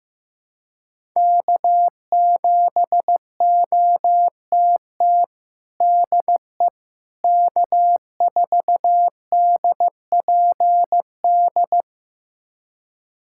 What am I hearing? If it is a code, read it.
Morse code, heard as a keyed tone, about 15 wpm, 700 Hz. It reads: K7OTT DE K4DPD
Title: USA Callsign DE Callsign - 5 Characters - 15wpm